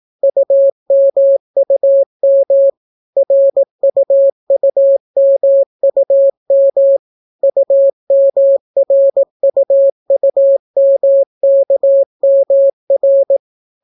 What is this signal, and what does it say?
Morse code, heard as a keyed tone, about 18 wpm, 550 Hz. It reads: UMUM RUUMUM UMRUUMKMR